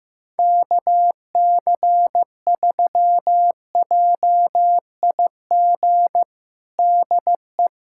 Morse code, heard as a keyed tone, about 16 wpm, 700 Hz. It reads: KC3JIG DE
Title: USA Callsign DE Callsign 15wpm